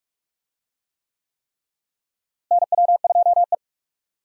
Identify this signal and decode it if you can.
Morse code, heard as a keyed tone, about 46 wpm, 700 Hz. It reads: NW2E
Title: USA Call Signs - 4 Characters - Repeated 3x - 45wpm